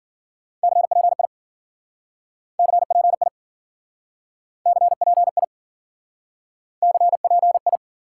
Morse code, heard as a keyed tone, about 50 wpm, 700 Hz. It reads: CPI CPI CPI CPI
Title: Speed-Racing Top 90 QSO Elements 40wpm